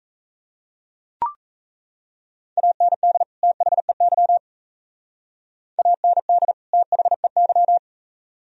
Morse code, heard as a keyed tone, about 36 wpm, 700 Hz. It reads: AND THEY AND THEY